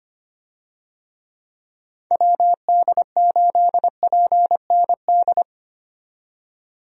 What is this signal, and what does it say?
Morse code, heard as a keyed tone, about 25 wpm, 700 Hz. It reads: WD8PND
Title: USA Call Signs - 6 Characters - 25wpm